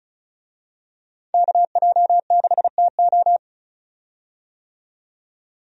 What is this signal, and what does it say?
Morse code, heard as a keyed tone, about 34 wpm, 700 Hz. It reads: KJ6TO
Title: USA Call Signs - 5 Characters - 35wpm